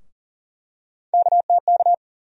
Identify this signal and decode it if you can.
Morse code, heard as a keyed tone, about 39 wpm, 700 Hz. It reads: KTK
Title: USA Call Sign Postfixes - 3 Letters 40wpm